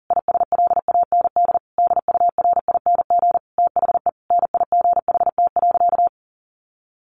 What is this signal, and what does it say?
Morse code, heard as a keyed tone, about 40 wpm, 700 Hz. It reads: ISLAND DURING THE NIGHT.
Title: Sentences from Top 1400 Words 40wpm